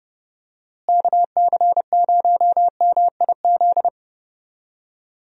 Morse code, heard as a keyed tone, about 30 wpm, 700 Hz. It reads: KC0MIZ
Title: USA Call Signs - 6 Characters - Repeated 2x - 30wpm